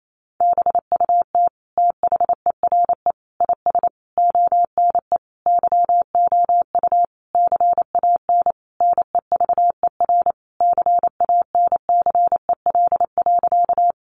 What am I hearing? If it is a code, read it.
Morse code, heard as a keyed tone, about 28 wpm, 700 Hz. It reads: BUT THERE IS ONE YOU CAN NEVER CANCEL.
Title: Sentences from Top 1200 Words 28wpm